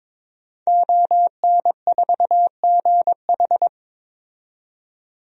Morse code, heard as a keyed tone, about 22 wpm, 700 Hz. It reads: ON4GH
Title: DXCC-100 Call Signs - 5 Characters - 22wpm